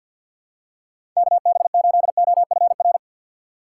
Morse code, heard as a keyed tone, about 52 wpm, 700 Hz. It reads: KB8OFR
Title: USA Call Signs - 6 Characters - 4x Character Spacing - 50wpm